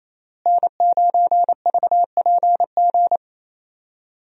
Morse code, heard as a keyed tone, about 28 wpm, 700 Hz. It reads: N9VPG
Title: USA Call Signs - 5 Characters - Repeated 2x - 28wpm